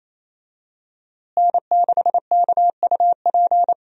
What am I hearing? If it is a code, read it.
Morse code, heard as a keyed tone, about 28 wpm, 700 Hz. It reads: N6KUP
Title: USA Call Signs - 5 Characters - Repeated 3x - 28wpm